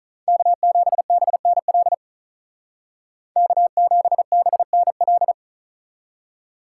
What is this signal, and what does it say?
Morse code, heard as a keyed tone, about 46 wpm, 700 Hz. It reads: K7BNL K7BNL